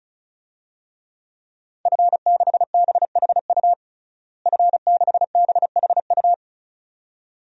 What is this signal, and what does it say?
Morse code, heard as a keyed tone, about 36 wpm, 700 Hz. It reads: F6BHU F6BHU
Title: DXCC-100 Call Signs - 5 Characters - Repeated 2x - 35wpm